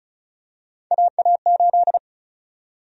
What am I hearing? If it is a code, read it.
Morse code, heard as a keyed tone, about 34 wpm, 700 Hz. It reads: AA8
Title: USA Call Sign Prefixes - 35wpm